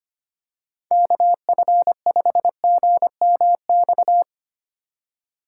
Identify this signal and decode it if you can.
Morse code, heard as a keyed tone, about 25 wpm, 700 Hz. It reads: KF5GMX
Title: USA Call Signs - 3x Character Spacing - 25wpm